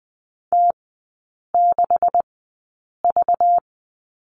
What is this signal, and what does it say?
Morse code, heard as a keyed tone, about 20 wpm, spaced as for 8 wpm, 700 Hz. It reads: T6V
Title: 191 Lesson 31e - Letter v - Three Characters 20wpm